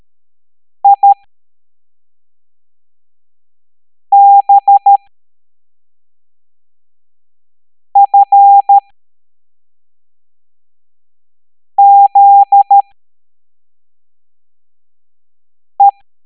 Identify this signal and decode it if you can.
Morse code, heard as a keyed tone, about 13 wpm, spaced as for 3 wpm, 800 Hz. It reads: IBFZE